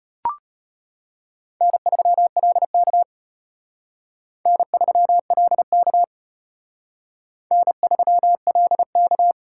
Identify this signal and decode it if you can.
Morse code, heard as a keyed tone, about 32 wpm, 700 Hz. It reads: N3LK N3LK N3LK